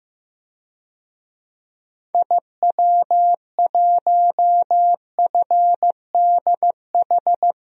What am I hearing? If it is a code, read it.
Morse code, heard as a keyed tone, about 15 wpm, 700 Hz. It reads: IW1FDH
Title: DXCC-100 Call Signs - 6 Characters - Repeated 3x - 15wpm